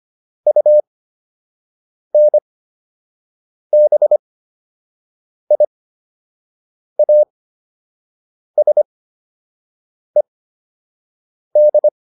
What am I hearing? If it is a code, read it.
Morse code, heard as a keyed tone, about 25 wpm, spaced as for 6 wpm, 600 Hz. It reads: UNBIASED